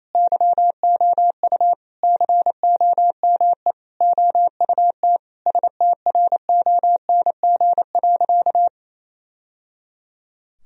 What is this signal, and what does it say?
Morse code, heard as a keyed tone, about 28 wpm, 700 Hz. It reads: YOU COME OUT STRONG.
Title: Sentences from Top 1000 Words - Review Part 3 - 28wpm